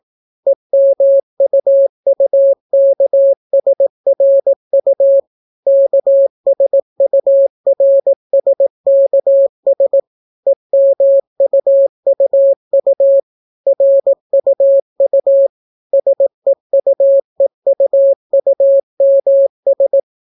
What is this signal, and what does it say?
Morse code, heard as a keyed tone, about 18 wpm, 550 Hz. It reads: EMUUKSRU KSURSKS EMUUU RUU SEUEUUMS